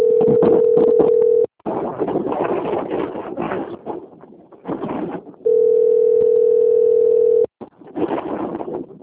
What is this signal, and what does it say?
Ringback tone (North America), 440 + 480 Hz.